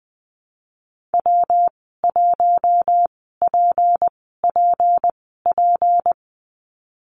Morse code, heard as a keyed tone, about 19 wpm, spaced as for 14 wpm, 700 Hz. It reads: W1PPP
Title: 173 Lesson 28g - Number 9 - Call Signs 20wpm